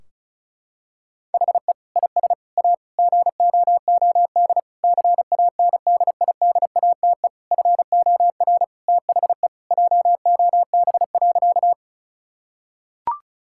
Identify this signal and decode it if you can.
Morse code, heard as a keyed tone, about 36 wpm, 700 Hz. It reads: HE IS A GOOD CANDIDATE FOR THE JOB.
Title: Sentences from Top 1600 Words 35wpm